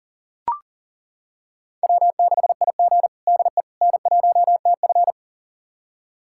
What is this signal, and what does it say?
Morse code, heard as a keyed tone, about 40 wpm, 700 Hz. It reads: W6IG DE N1TF
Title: USA Callsign DE Callsign - 4 Characters - 40wpm